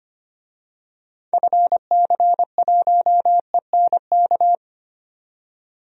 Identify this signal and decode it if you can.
Morse code, heard as a keyed tone, about 25 wpm, 700 Hz. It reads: FC1ENK
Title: DXCC-100 Call Signs - 6 Characters - 25wpm